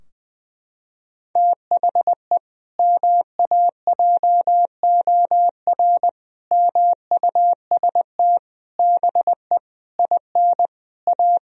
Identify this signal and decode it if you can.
Morse code, heard as a keyed tone, about 20 wpm, 700 Hz. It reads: THE MAJOR MUST BE IN A
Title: Sentences from Top 1000 Words - Review Part 4 - 20wpm